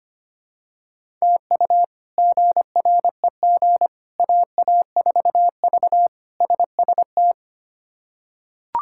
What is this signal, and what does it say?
Morse code, heard as a keyed tone, about 25 wpm, 700 Hz. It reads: TU GREG AA4V SST
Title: K1USN SST QSO Practice - 25wpm